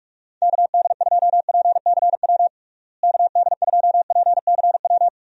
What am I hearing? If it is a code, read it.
Morse code, heard as a keyed tone, about 46 wpm, 700 Hz. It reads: KD2PCW KD2PCW
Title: USA Call Signs - 6 Characters - Repeated 2x - 45wpm